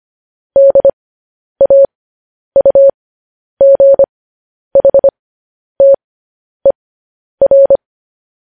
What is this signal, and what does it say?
Morse code, heard as a keyed tone, about 25 wpm, spaced as for 10 wpm, 550 Hz. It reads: DAUGHTER
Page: news headlines in morse code at 10 wpm